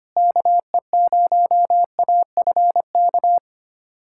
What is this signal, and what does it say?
Morse code, heard as a keyed tone, about 25 wpm, 700 Hz. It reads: KE0AFK
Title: USA Callsign DE Callsign 25wpm